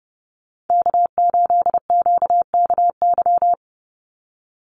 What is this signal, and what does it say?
Morse code, heard as a keyed tone, about 30 wpm, 700 Hz. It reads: K8QKY
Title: Call Signs Repeated 3x - 30wpm